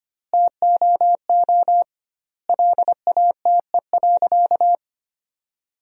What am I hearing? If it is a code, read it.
Morse code, heard as a keyed tone, about 25 wpm, 700 Hz. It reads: TOO LATE.
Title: 2x Word-Spacing - Sentences from Top 300 Words - 25wpm